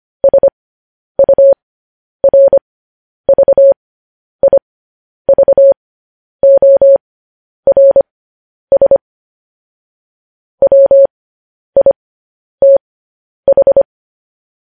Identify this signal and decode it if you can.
Morse code, heard as a keyed tone, about 25 wpm, spaced as for 10 wpm, 550 Hz. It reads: SURVIVORS WITH
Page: news-headlines-in-morse-code-at-10-wpm